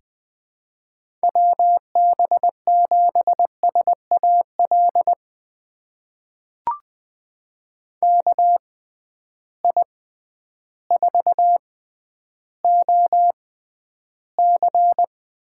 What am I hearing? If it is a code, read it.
Morse code, heard as a keyed tone, about 20 wpm, spaced as for 11 wpm, 700 Hz. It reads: WB7SAL KI4OC
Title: USA Call Signs - 6x Character Spacing - 20wpm